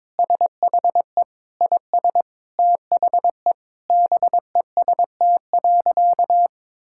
Morse code, heard as a keyed tone, about 22 wpm, 700 Hz. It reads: SHE IS THE BEST.
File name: practice / 22wpm / Sentences with 4-Word Phrases - 22wpm